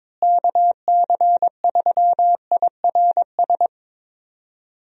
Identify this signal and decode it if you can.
Morse code, heard as a keyed tone, about 21 wpm, 700 Hz. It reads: KC3IRS
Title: USA Call Signs - 6 Characters - Repeated 3x - 22wpm